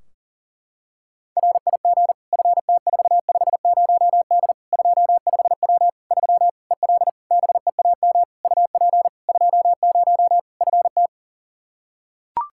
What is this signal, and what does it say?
Morse code, heard as a keyed tone, about 40 wpm, 700 Hz. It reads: RIG FT450D 25W 3 EL BEAM UP 20 FT
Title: Practice QSO - Exchange 2 - Info - 40wpm